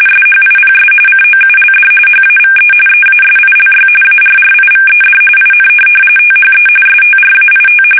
RTTY, about 45.45 baud. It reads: SHIPS COULD HAVE EASILY ARRIVED IN THE STRAIT
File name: fsk850 / USS-Massachusetts